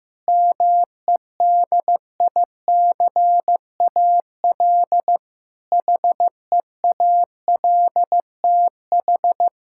Morse code, heard as a keyed tone, about 15 wpm, 700 Hz. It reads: MEDICAL HEALTH
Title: Practice QSO - Exchange 3 - More Info - 15wpm